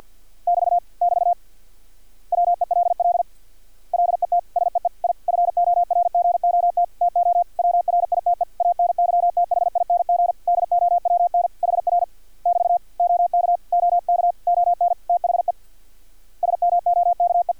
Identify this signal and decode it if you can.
Morse code, heard as a keyed tone, about 50 wpm, 700 Hz. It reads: <BT> <BT> WELL LET SEE I FORGOT TO WRITE ANYTHING DOWN HR <BT> OK OK ON THE SMOKE
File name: archived / CRSnet